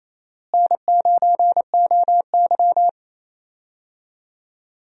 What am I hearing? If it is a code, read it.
Morse code, heard as a keyed tone, about 28 wpm, 700 Hz. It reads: N9OY
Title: USA Call Signs - 4 Characters - Repeated 2x - 28wpm